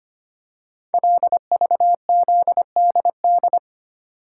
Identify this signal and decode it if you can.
Morse code, heard as a keyed tone, about 25 wpm, 700 Hz. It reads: LVZDD
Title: Code Groups - 5 Letters - 25wpm